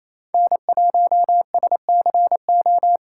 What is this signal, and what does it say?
Morse code, heard as a keyed tone, about 36 wpm, 700 Hz. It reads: N1SCO